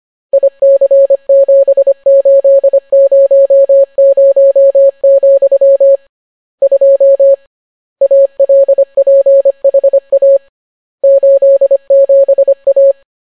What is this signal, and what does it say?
Morse code, heard as a keyed tone, about 25 wpm, 550 Hz. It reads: IC7800, 2 ALPHA 87A